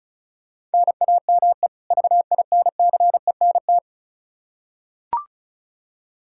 Morse code, heard as a keyed tone, about 36 wpm, 700 Hz. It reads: NAME VINCENT
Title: Practice QSO - Exchange 1 - Beginning - 35wpm